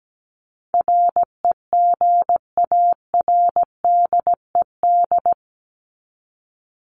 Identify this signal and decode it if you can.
Morse code, heard as a keyed tone, about 17 wpm, 700 Hz. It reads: REGARDED
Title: ED Suffix - Top 100 Words with Base Words 17wpm